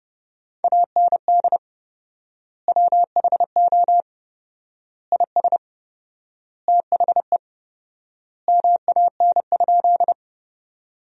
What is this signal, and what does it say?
Morse code, heard as a keyed tone, about 32 wpm, 700 Hz. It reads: AND WHO IS THE MAN?
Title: 4x Word-Spacing - Sentences from Top 200 Words - 30wpm